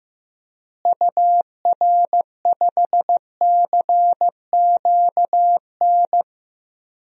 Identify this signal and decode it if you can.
Morse code, heard as a keyed tone, about 16 wpm, 700 Hz. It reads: UR5CQN
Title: DXCC-100 Call Signs - 6 Characters - Repeated 3x - 15wpm